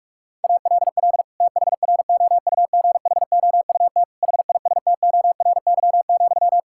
Morse code, heard as a keyed tone, about 46 wpm, 700 Hz. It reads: ALL THROUGHOUT HISTORY,
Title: Sentences from Top 1000 Words - Review Part 5 - 45wpm